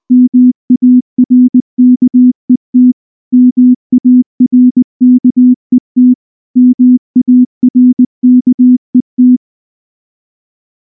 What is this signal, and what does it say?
Morse code, heard as a keyed tone, about 19 wpm, 250 Hz. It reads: MARKET MARKET MARKET
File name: morse / morse-training-20WPM-10